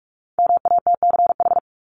Morse code, heard as a keyed tone, about 44 wpm, 700 Hz. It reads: MATCH
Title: Top 1501-1750 Words 45wpm